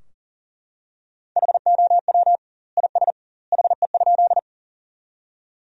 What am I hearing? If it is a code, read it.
Morse code, heard as a keyed tone, about 40 wpm, 700 Hz. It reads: HOW IS HE?
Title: 2x Word-Spacing - Sentences from Top 100 Words - 40wpm